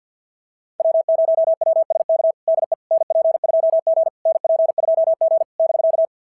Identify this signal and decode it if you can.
Morse code, heard as a keyed tone, about 46 wpm, 650 Hz. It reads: W0WSK DE NP2G NP2G <BK>
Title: SOTA QSO Practice 50wpm